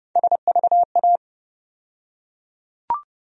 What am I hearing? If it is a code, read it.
Morse code, heard as a keyed tone, about 30 wpm, 700 Hz. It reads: SVA
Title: USA Call Sign Postfixes - 3 Letters 30wpm